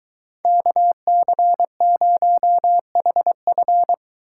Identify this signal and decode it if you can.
Morse code, heard as a keyed tone, about 23 wpm, 700 Hz. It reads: KC0HF